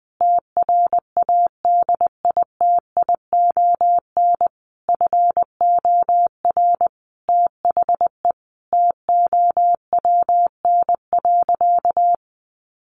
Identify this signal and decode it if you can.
Morse code, heard as a keyed tone, about 20 wpm, 700 Hz. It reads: TRADITION FOR THE TOWN.
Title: Sentences from Top 1500 Words 20wpm